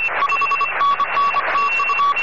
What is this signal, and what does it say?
Morse code, heard as a keyed tone, about 26 wpm, 1120 Hz, with noise. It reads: 5NN TU